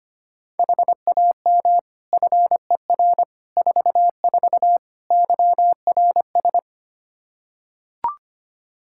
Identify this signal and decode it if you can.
Morse code, heard as a keyed tone, about 25 wpm, 700 Hz. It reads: HAM FER 44 YRS
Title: Practice QSO - Exchange 3 - More Info - 25wpm